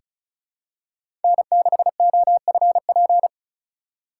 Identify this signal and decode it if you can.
Morse code, heard as a keyed tone, about 34 wpm, 700 Hz. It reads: N6OFP
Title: USA Call Signs - 5 Characters - 2x Character Spacing - 35wpm